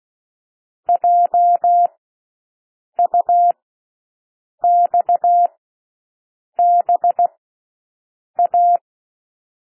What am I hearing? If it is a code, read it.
Morse code, heard as a keyed tone, about 16 wpm, spaced as for 6 wpm, 700 Hz. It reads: JUXBA
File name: cwmp3 / L1 SOUND LETTERS 16 wpm